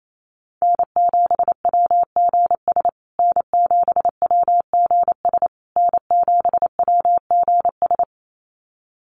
Morse code, heard as a keyed tone, about 28 wpm, 700 Hz. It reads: N7WGS N7WGS N7WGS
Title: USA Call Signs Repeated 3x 28wpm